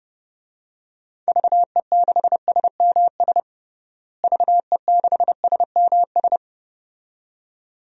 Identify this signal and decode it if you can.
Morse code, heard as a keyed tone, about 32 wpm, 700 Hz. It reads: VE6SMS VE6SMS
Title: DXCC-100 Call Signs - 6 Characters - Repeated 2x - 30wpm